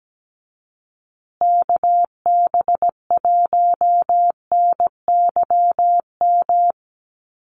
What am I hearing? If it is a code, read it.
Morse code, heard as a keyed tone, about 17 wpm, 700 Hz. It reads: KB1NYM
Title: USA Call Signs Repeated 2x 17wpm